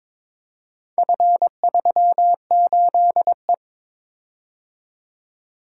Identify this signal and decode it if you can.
Morse code, heard as a keyed tone, about 22 wpm, 700 Hz. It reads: F38E
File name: practice / 22wpm / Four Letter-Number Combinations - 22wpm